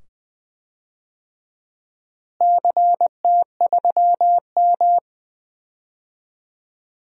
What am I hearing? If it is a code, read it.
Morse code, heard as a keyed tone, about 20 wpm, 700 Hz. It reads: CT3M
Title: DXCC-100 Call Signs - 4 Characters - Repeated 2x - 20wpm